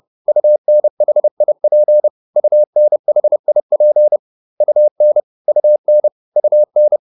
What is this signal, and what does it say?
Morse code, heard as a keyed tone, about 30 wpm, 600 Hz. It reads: UNHIP UNHIP UN UN UN